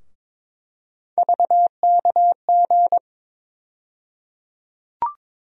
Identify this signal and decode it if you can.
Morse code, heard as a keyed tone, about 22 wpm, 700 Hz. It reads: VKG
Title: USA Call Sign Postfixes - 3 Letters 22wpm